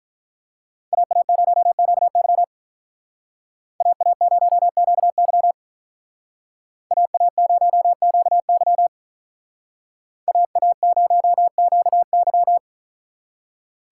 Morse code, heard as a keyed tone, about 45 wpm, 700 Hz. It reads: AA0QY AA0QY AA0QY AA0QY